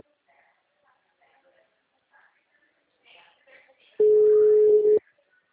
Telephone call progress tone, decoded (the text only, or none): ringback tone